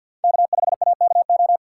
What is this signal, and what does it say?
Morse code, heard as a keyed tone, about 53 wpm, 700 Hz. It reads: K5AKO
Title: USA Call Signs - 5 Characters - 6x Character Spacing - 50wpm